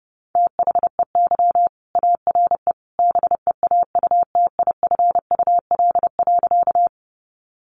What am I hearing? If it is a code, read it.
Morse code, heard as a keyed tone, about 30 wpm, 700 Hz. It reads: THEY ARE BEAUTIFUL.
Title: Sentences from Top 900 Words 30wpm